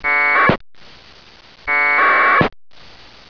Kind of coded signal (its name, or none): packet radio (AX.25, as in APRS)